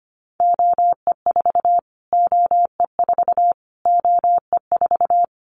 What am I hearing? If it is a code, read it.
Morse code, heard as a keyed tone, about 25 wpm, 700 Hz. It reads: OE4 OE4 OE4
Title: Callsign Prefixes for Top 25 DXCC - Repeated 3x - 25wpm